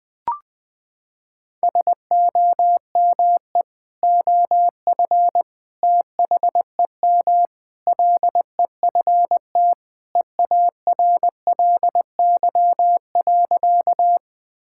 Morse code, heard as a keyed tone, about 20 wpm, 700 Hz. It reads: SOME OF THEM LEFT EARLY.
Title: Sentences with 3-Word Phrases - 20wpm